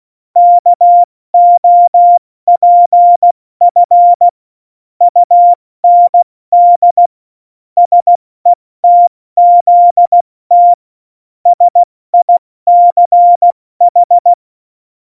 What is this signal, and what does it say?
Morse code, heard as a keyed tone, about 16 wpm, 700 Hz. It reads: KOPF UND SETZT SICH
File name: Morsekurs / Witze / Witz 19. Juni 2018, 70 F80